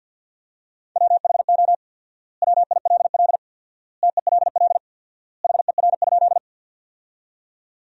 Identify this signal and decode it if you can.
Morse code, heard as a keyed tone, about 48 wpm, 700 Hz. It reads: WHO WILL TELL HER?
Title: 4x Word-Spacing - Sentences from Top 200 Words - 50wpm